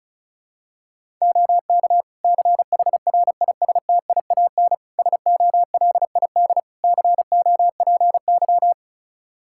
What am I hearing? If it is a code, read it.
Morse code, heard as a keyed tone, about 36 wpm, 700 Hz. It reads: OK CHRISTIAN SOLID COPY
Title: Practice QSO - Exchange 3 - More Info - 35wpm